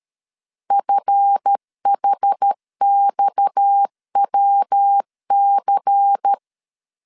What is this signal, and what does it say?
Morse code, heard as a keyed tone, about 13 wpm, 800 Hz. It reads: FHXWC
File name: LEZIONI / 10 SIGNS / 10A SLASH 12wpm 2314